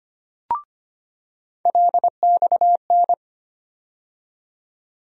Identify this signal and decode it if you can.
Morse code, heard as a keyed tone, about 25 wpm, 700 Hz. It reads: LXN
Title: USA Call Sign Postfixes - 3 Letters 25wpm